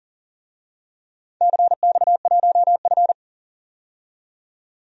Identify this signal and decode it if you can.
Morse code, heard as a keyed tone, about 42 wpm, 700 Hz. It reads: CX1F
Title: DXCC-100 Call Signs - 4 Characters - 40wpm